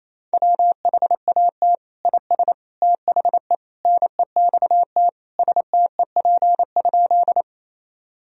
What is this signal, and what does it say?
Morse code, heard as a keyed tone, about 27 wpm, 690 Hz. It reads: WHAT IS THE NEXT STEP?
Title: Sentences from Top 1000 Words - Review Part 2 - 28wpm